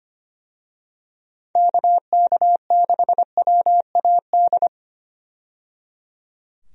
Morse code, heard as a keyed tone, about 25 wpm, 700 Hz. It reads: KK6WAD
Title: USA Call Signs - 6 Characters - 25wpm